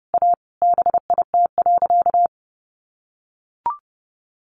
Morse code, heard as a keyed tone, about 30 wpm, 700 Hz. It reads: A BIT.
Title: Sentences from Top 900 Words 30wpm